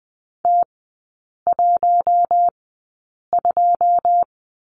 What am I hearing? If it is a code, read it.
Morse code, heard as a keyed tone, about 20 wpm, spaced as for 8 wpm, 700 Hz. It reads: T12
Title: 133 Lesson 22e - Question Mark - Three Characters 20wpm